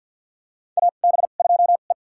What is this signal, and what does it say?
Morse code, heard as a keyed tone, about 50 wpm, spaced as for 36 wpm, 700 Hz. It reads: AB2E
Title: USA Call Signs - 4 Characters - 2x Character Spacing - 50wpm